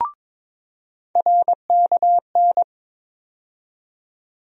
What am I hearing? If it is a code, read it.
Morse code, heard as a keyed tone, about 22 wpm, 700 Hz. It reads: RKN
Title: USA Call Sign Postfixes - 3 Letters 22wpm